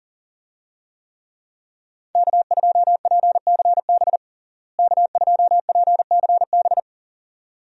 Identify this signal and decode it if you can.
Morse code, heard as a keyed tone, about 40 wpm, 700 Hz. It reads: K2PCB K2PCB
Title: USA Call Signs - 5 Characters - Repeated 2x - 40wpm